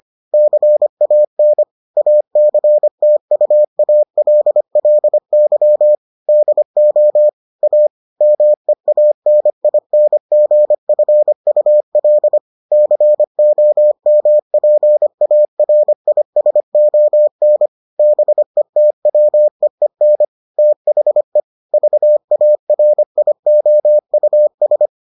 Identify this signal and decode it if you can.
Morse code, heard as a keyed tone, about 25 wpm, 600 Hz. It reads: CAN ACTUALLY DO A MEANINGFUL COMPARISON BETWEEN THE VARIOUS